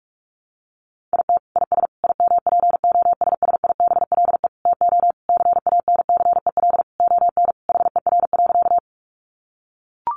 Morse code, heard as a keyed tone, about 44 wpm, 700 Hz. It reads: IT IS IMPOSSIBLE TO CANCEL ON HER.
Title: Sentences from Top 1200 Words 45wpm